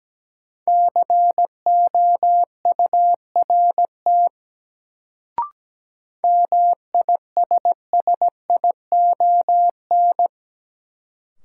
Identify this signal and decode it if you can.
Morse code, heard as a keyed tone, about 17 wpm, 700 Hz. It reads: COURT MISSION
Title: Sentences from Top 1600 Words - New Word Drill Alt - 17wpm